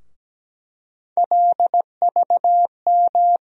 Morse code, heard as a keyed tone, about 17 wpm, 700 Hz. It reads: LVM
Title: USA Call Sign Postfixes - 3 Letters 17wpm